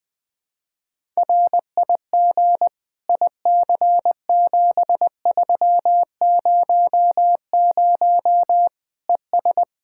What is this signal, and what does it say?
Morse code, heard as a keyed tone, about 20 wpm, 700 Hz. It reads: RIG IC7300 ES
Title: Practice QSO - Exchange 2 - Info - 20wpm